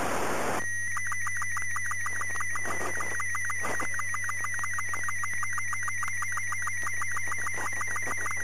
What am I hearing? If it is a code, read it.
SSTV (slow-scan television), Robot 36.
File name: ISS / na1iss 15oct08 sstv